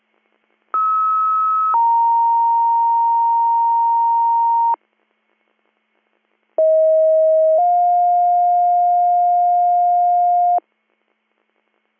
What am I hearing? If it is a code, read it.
Two-tone page, heard as a one-second tone, then a three-second tone, about 1290 then 930 Hz, about 640 then 710 Hz. It is heard from 0.7 to 10.6 s.